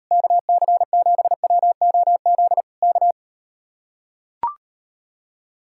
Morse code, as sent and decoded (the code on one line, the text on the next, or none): -.- -.-. --... .-- --- --.. / -.-
KC7WOZ K